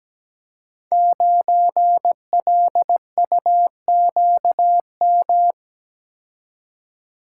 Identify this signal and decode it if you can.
Morse code, heard as a keyed tone, about 17 wpm, 700 Hz. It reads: 9LUQM
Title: Code Groups - 5 Letter:Numbers - 17wpm